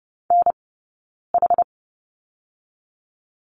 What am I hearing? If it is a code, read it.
Morse code, heard as a keyed tone, about 30 wpm, 700 Hz. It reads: N H
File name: Lesson 18d - Letter m - Two Characters 30wpm